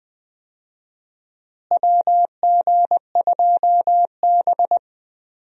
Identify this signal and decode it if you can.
Morse code, heard as a keyed tone, about 20 wpm, 700 Hz. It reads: WG2B
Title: USA Call Signs - 4 Characters - Repeated 2x - 20wpm